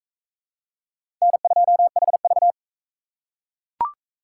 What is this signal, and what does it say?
Morse code, heard as a keyed tone, about 37 wpm, 700 Hz. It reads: N2HV